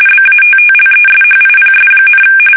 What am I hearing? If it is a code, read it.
RTTY, about 45.45 baud. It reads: BATTLESHIPS,
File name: fsk850 / battleshipnj